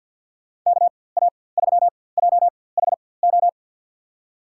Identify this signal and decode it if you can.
Morse code, heard as a keyed tone, about 53 wpm, spaced as for 24 wpm, 700 Hz. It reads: KA3JHO
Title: USA Call Signs - 4x Character Spacing - 50wpm